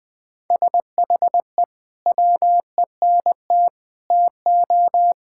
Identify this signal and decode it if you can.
Morse code, heard as a keyed tone, about 21 wpm, 700 Hz. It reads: SHE WENT TO
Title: Sentences with 2-Word Phrases - 20wpm